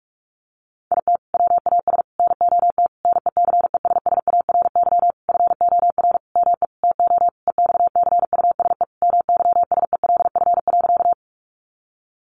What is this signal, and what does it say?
Morse code, heard as a keyed tone, about 44 wpm, 700 Hz. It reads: IT WAS NOT NECESSARY FOR ME TO EXCUSE MYSELF.
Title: Sentences from Top 1300 Words 45wpm